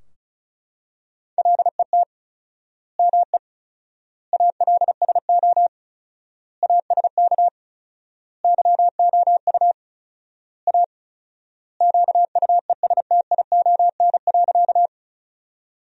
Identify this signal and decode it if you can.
Morse code, heard as a keyed tone, about 34 wpm, 700 Hz. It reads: LET ME ALSO ASK YOU A QUESTION.
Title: 4x Word-Spacing - Sentences from Top 400 Words - 35wpm